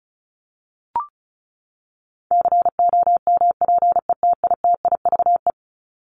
Morse code, heard as a keyed tone, about 34 wpm, 700 Hz. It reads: COMPETITIVE
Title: Top 1001-1250 Words 35wpm